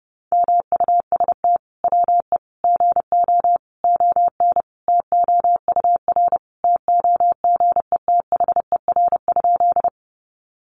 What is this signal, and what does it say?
Morse code, heard as a keyed tone, about 30 wpm, 700 Hz. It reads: MUST WE GO ON TOUR TOGETHER?
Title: Sentences from Top 1400 Words 30wpm